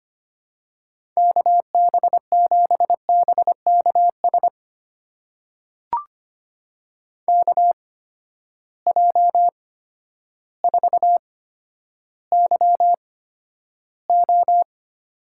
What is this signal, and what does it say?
Morse code, heard as a keyed tone, about 25 wpm, spaced as for 11 wpm, 700 Hz. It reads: KB7BKS KJ4YO
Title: USA Call Signs - 6 Characters - 8x Character Spacing - 25wpm